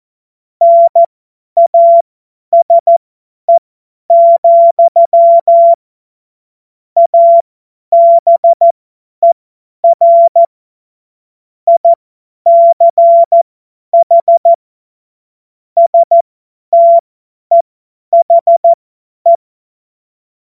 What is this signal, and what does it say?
Morse code, heard as a keyed tone, about 14 wpm, spaced as for 10 wpm, 680 Hz. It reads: NASE, ABER ICH STEHE
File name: Morsekurs / Witze / Witz 12. April 2022 50 F70